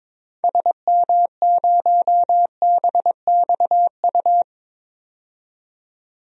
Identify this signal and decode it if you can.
Morse code, heard as a keyed tone, about 22 wpm, 700 Hz. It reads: SM0BXU
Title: DXCC-100 Call Signs - 6 Characters - 22wpm